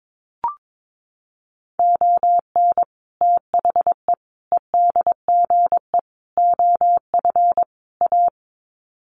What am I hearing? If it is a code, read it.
Morse code, heard as a keyed tone, about 22 wpm, 700 Hz. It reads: ON THE EDGE OF A
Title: Sets of 5 Words Encore2 22wpm